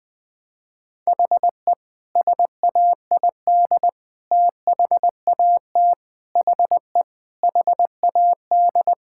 Morse code, heard as a keyed tone, about 20 wpm, 700 Hz. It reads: HE SAID THAT HE HAD
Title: Sentences with 3-Word Phrases - 20wpm